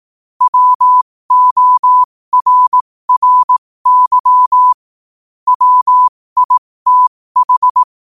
Morse code, heard as a keyed tone, about 18 wpm, spaced as for 15 wpm, 1000 Hz. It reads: WORRY WITH
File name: qotdfiles / 15wpm